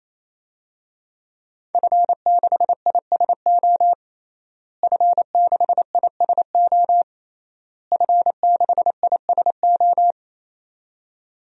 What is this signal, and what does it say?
Morse code, heard as a keyed tone, about 28 wpm, 700 Hz. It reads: F6ISO F6ISO F6ISO
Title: DXCC-100 Call Signs - 5 Characters - Repeated 3x - 28wpm